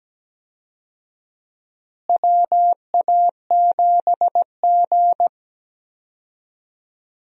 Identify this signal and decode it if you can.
Morse code, heard as a keyed tone, about 17 wpm, 700 Hz. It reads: WA7G
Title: USA Call Signs - 4 Characters - Repeated 2x - 17wpm